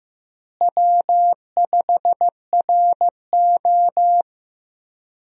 Morse code, heard as a keyed tone, about 15 wpm, 700 Hz. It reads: W5RO
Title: USA Call Signs - 4 Characters - Repeated 2x - 15wpm